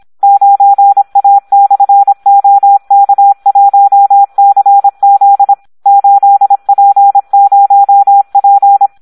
Morse code, heard as a keyed tone, about 26 wpm, 800 Hz. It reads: 9A/OK1CZ 8P0P
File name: Audio practica CW / SD 1 26